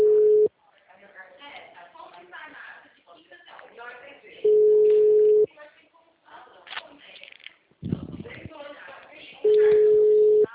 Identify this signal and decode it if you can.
Ringback tone (Europe, 425 Hz), with speech.